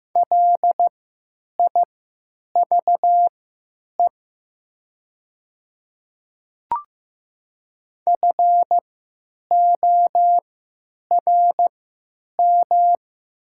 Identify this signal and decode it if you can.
Morse code, heard as a keyed tone, about 15 wpm, spaced as for 9 wpm, 700 Hz. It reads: LIVE FORM
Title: 3x Character Spacing - Top 500 Words - 15wpm